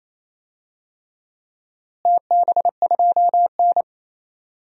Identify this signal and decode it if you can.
Morse code, heard as a keyed tone, about 28 wpm, 700 Hz. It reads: TB2N
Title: DXCC-100 Call Signs - 4 Characters - Repeated 3x - 28wpm